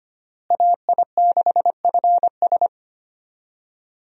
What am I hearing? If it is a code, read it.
Morse code, heard as a keyed tone, about 25 wpm, 700 Hz. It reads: AI6FS
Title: USA Call Signs - 5 Characters - Repeated 3x - 25wpm